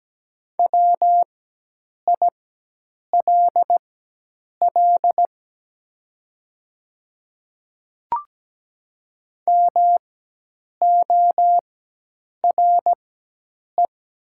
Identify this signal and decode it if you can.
Morse code, heard as a keyed tone, about 17 wpm, spaced as for 8 wpm, 700 Hz. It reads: WILL MORE
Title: 4x Character Spacing - Top 200 Words - 17wpm